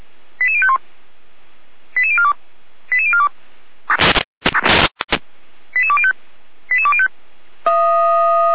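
Five-tone selective call (ZVEI), heard as quick runs of five tones, tones of about 70 ms. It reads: 88051 88042 88042 88205 88205
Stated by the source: Five-tone ZVEI selective call received on an SDR.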